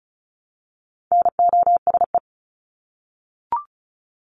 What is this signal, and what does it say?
Morse code, heard as a keyed tone, about 34 wpm, 700 Hz. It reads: NOSE
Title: Top 1501-1750 Words 35wpm